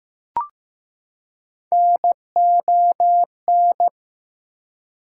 Morse code, heard as a keyed tone, about 15 wpm, 700 Hz. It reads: NON
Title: Prefixes and Suffixes 15wpm